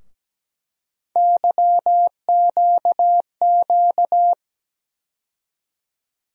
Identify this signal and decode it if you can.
Morse code, heard as a keyed tone, about 17 wpm, 700 Hz. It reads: YQQ
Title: USA Call Sign Postfixes - 3 Letters 17wpm